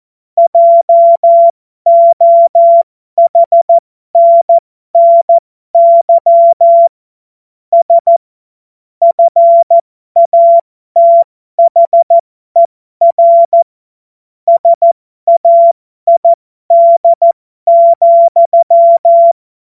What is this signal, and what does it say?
Morse code, heard as a keyed tone, about 14 wpm, spaced as for 12 wpm, 680 Hz. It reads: JOHNNY S FATHER SAID,
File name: Morsekurs / Witze